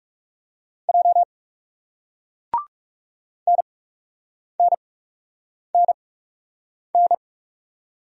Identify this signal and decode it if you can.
Morse code, heard as a keyed tone, about 40 wpm, spaced as for 8 wpm, 700 Hz. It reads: J NNNN